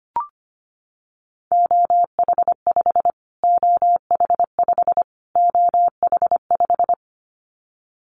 Morse code, heard as a keyed tone, about 25 wpm, 700 Hz. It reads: OH5 OH5 OH5
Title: Callsign Prefixes for Top 25 DXCC - Repeated 3x - 25wpm